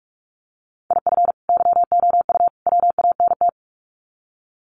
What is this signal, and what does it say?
Morse code, heard as a keyed tone, about 44 wpm, 700 Hz. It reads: IF YOU WANT
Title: Sets of 3 Words 45wpm